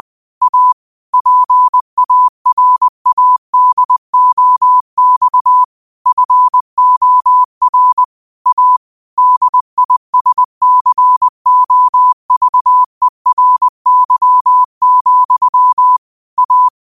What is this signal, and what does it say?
Morse code, heard as a keyed tone, about 20 wpm, 1000 Hz. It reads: A PARADOX FOR A DISCOVERY, A
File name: qotdfiles / 20wpm